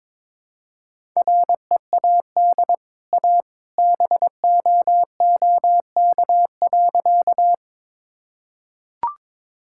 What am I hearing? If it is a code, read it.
Morse code, heard as a keyed tone, about 22 wpm, 690 Hz. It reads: READ A BOOK.
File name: practice / 22wpm / Easy Sentences with Hints - 22wpm